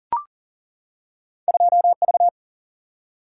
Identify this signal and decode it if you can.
Morse code, heard as a keyed tone, about 42 wpm, 700 Hz. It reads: 2V
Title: Single-Letter Single-Number 40wpm